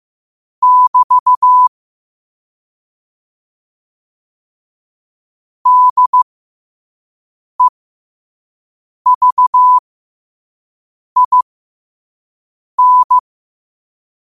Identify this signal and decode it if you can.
Morse code, heard as a keyed tone, about 15 wpm, spaced as for 5 wpm, 1000 Hz. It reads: <BT> DEVIN
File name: qotdfiles / 5wpm